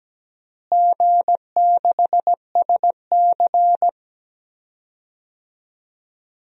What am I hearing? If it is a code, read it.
Morse code, heard as a keyed tone, about 17 wpm, 700 Hz. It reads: G6SC
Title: DXCC-100 Call Signs - 4 Characters - Repeated 3x - 17wpm